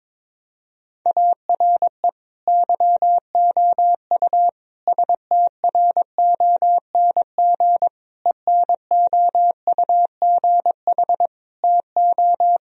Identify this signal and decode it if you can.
Morse code, heard as a keyed tone, about 22 wpm, 700 Hz. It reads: ARE YOU STRONG ENOUGH TO
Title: Sentences from Top 1000 Words - Review Part 3 - 22wpm